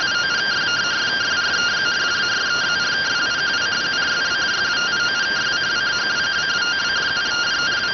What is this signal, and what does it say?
Two-tone teleprinter signal, about 170 Hz shift.